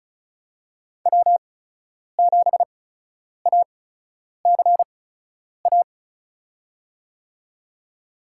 Morse code, heard as a keyed tone, about 36 wpm, spaced as for 10 wpm, 700 Hz. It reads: W7ACA
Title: USA Call Signs - 5 Characters - 8x Character Spacing - 35wpm